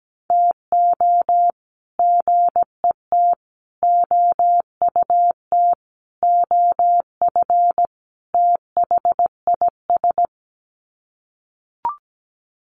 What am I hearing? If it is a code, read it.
Morse code, heard as a keyed tone, about 17 wpm, 700 Hz. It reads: TO GET OUT OF THIS
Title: Sets of 5 Words Encore2 17wpm